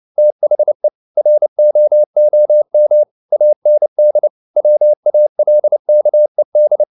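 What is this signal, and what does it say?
Morse code, heard as a keyed tone, about 29 wpm, 600 Hz. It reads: THE ROOM AND WALKED
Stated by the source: Morse code at 29 wpm, sending Ken Follett's novel Never in English, from CW med Gnister.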